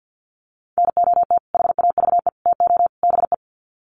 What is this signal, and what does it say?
Morse code, heard as a keyed tone, about 50 wpm, 700 Hz. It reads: NOT HAVE TO BE
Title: Sets of 4 Words 50wpm